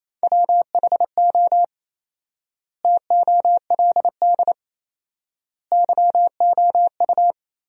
Morse code, heard as a keyed tone, about 28 wpm, 700 Hz. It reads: WHO TOLD YOU
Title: 4x Word-Spacing - Sentences from Top 400 Words - 28wpm